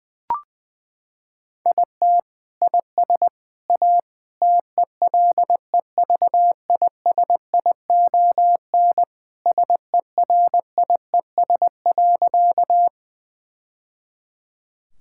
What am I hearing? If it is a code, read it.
Morse code, heard as a keyed tone, about 20 wpm, 700 Hz. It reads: IT IS A TELEVISION SERIES.
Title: Sentences from Top 1000 Words - Review Part 5 - 20wpm